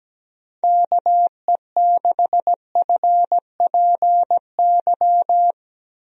Morse code, heard as a keyed tone, about 17 wpm, 700 Hz. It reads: KE6FPY
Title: USA Call Signs - 6 Characters - Repeated 3x - 17wpm